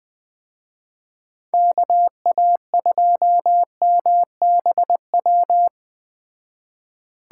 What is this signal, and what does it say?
Morse code, heard as a keyed tone, about 20 wpm, 700 Hz. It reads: KA2MBW
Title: Speed Racing - USA 2x3 Call Signs 17wpm